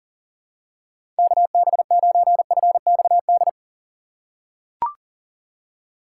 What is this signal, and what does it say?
Morse code, heard as a keyed tone, about 40 wpm, 700 Hz. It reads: KB9FXD
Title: USA Call Signs - 6 Characters - 3x Character Spacing - 40wpm